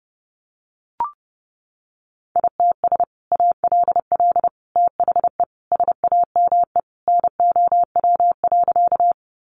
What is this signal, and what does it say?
Morse code, heard as a keyed tone, about 30 wpm, 700 Hz. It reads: ITS ALL THE SAME NOW.
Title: Sentences from Top 600 Words 30wpm